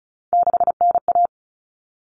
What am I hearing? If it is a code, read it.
Morse code, heard as a keyed tone, about 34 wpm, 700 Hz. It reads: 6NA
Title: Common Cut Numbers - 3-Digit 35wpm